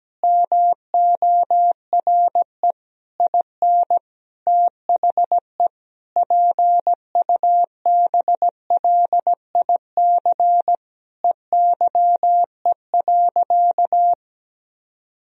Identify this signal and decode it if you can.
Morse code, heard as a keyed tone, about 17 wpm, 700 Hz. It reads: MORE IN THE PUBLIC EYE.
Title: Sentences from Top 1000 Words - Review Part 3 - 17wpm